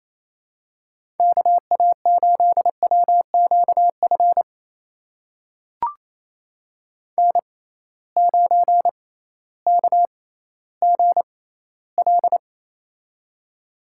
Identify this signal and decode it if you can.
Morse code, heard as a keyed tone, about 28 wpm, spaced as for 15 wpm, 700 Hz. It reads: KA8WQF N9KGL
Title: USA Call Signs - 6x Character Spacing - 28wpm